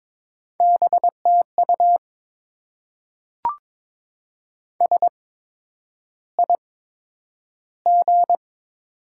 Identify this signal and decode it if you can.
Morse code, heard as a keyed tone, about 22 wpm, spaced as for 9 wpm, 700 Hz. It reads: BTU SIG
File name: Top 90 QSO Elements - 8x Character Spacing - 22wpm